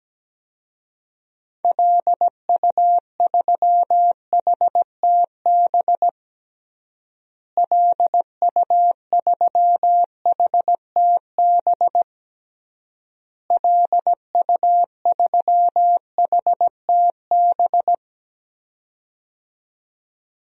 Morse code, heard as a keyed tone, about 17 wpm, 700 Hz. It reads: LU3HTB LU3HTB LU3HTB
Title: DXCC-100 Call Signs - 6 Characters - Repeated 3x - 17wpm